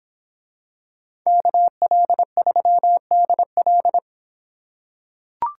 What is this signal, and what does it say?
Morse code, heard as a keyed tone, about 26 wpm, 700 Hz. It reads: KL3DL